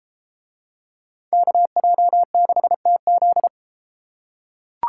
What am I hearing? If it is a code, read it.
Morse code, heard as a keyed tone, about 34 wpm, 700 Hz. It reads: KJ6TZ